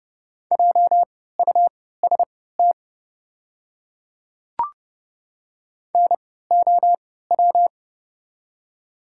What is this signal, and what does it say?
Morse code, heard as a keyed tone, about 30 wpm, spaced as for 17 wpm, 700 Hz. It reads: JUST NOW
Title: 3x Character Spacing - Top 300 Words - 30wpm